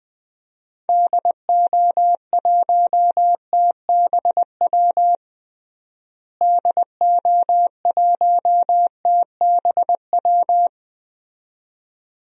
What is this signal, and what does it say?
Morse code, heard as a keyed tone, about 20 wpm, 700 Hz. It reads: DO1TBW DO1TBW
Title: DXCC-100 Call Signs - 6 Characters - Repeated 2x - 20wpm